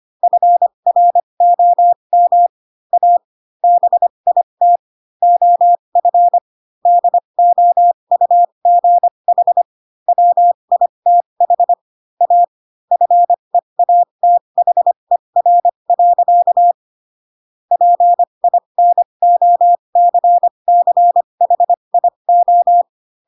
Morse code, heard as a keyed tone, about 25 wpm, spaced as for 22 wpm, 700 Hz. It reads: FROM A BIT OF DOUGH WITH A FEATHER. PINOCCHIO